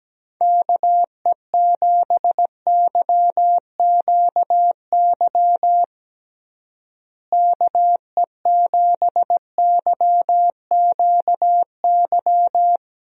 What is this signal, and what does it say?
Morse code, heard as a keyed tone, about 17 wpm, 700 Hz. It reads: KE7YQY KE7YQY
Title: USA Call Signs - 6 Characters - Repeated 3x - 17wpm